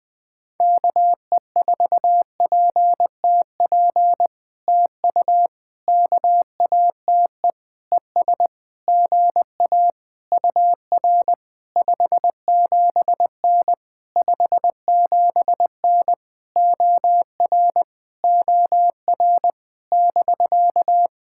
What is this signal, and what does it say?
Morse code, heard as a keyed tone, about 20 wpm, 700 Hz. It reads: KE4PTP TU KATE ES GA UR 57N 57N OR OR <BK>
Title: POTA QSO Practice - 20wpm